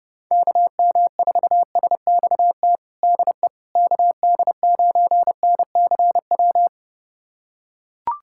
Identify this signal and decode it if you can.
Morse code, heard as a keyed tone, about 30 wpm, 700 Hz. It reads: KM4SXT DE KD9NCW
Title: USA Callsign DE Callsign - 6 Characters - 30wpm